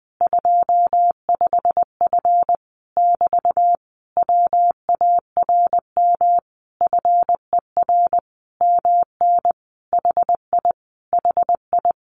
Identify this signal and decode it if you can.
Morse code, heard as a keyed tone, about 20 wpm, 700 Hz. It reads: 25F <BT> WARM FER MN HI HI
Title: Common QSO Phrases Encore 20wpm